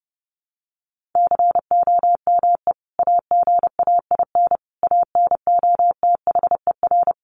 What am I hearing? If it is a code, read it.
Morse code, heard as a keyed tone, about 30 wpm, 700 Hz. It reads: COME AGAIN ANOTHER
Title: Sentences from Top 600 Words 30wpm